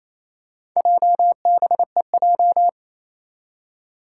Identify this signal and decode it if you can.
Morse code, heard as a keyed tone, about 28 wpm, 700 Hz. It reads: JBEJ
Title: Four Challenging Letter-Number Combinations 28wpm